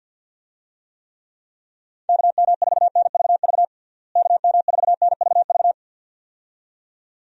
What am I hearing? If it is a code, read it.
Morse code, heard as a keyed tone, about 51 wpm, 700 Hz. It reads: KM4NVV KM4NVV
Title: USA Call Signs - 6 Characters - Repeated 2x - 50wpm